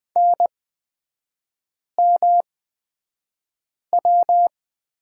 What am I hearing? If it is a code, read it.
Morse code, heard as a keyed tone, about 20 wpm, spaced as for 5 wpm, 700 Hz. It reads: NMW
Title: 250 Lesson 40e - BK - Three Characters 20wpm